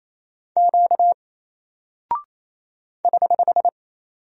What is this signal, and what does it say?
Morse code, heard as a keyed tone, about 28 wpm, 700 Hz. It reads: Q <HH>
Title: Single Character (Letter:Number:Punctuation:Prosign) Alt 28wpm